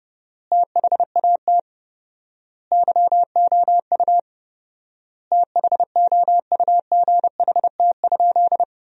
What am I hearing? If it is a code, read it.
Morse code, heard as a keyed tone, about 32 wpm, 700 Hz. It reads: THAT YOU THOUGHT?
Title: 4x Word-Spacing - Sentences from Top 300 Words - 30wpm